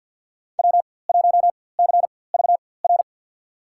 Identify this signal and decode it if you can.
Morse code, heard as a keyed tone, about 50 wpm, spaced as for 23 wpm, 700 Hz. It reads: W1CVR
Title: USA Call Signs - 5 Characters - 4x Character Spacing - 50wpm